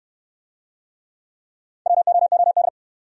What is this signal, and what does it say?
Morse code, heard as a keyed tone, about 65 wpm, 700 Hz. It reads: WOOD